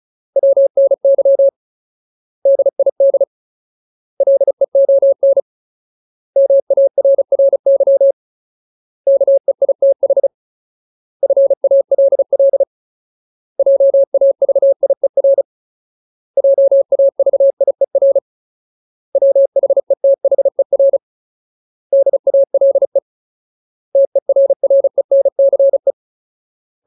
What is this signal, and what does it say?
Morse code, heard as a keyed tone, about 34 wpm, 550 Hz. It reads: WNY DID LEON MARRY KEITH FALL JAVIER JAVIER WHETHER DALE TERRENCE
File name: Ultimate Word List / 35WPM Ultimate